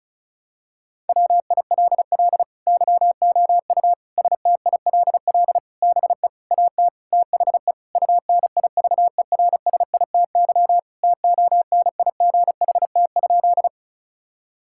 Morse code, heard as a keyed tone, about 36 wpm, 700 Hz. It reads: WILL YOU STILL BE AT THE UNIVERSITY TONIGHT?
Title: Sentences from Top 1000 Words - Review Part 5 - 35wpm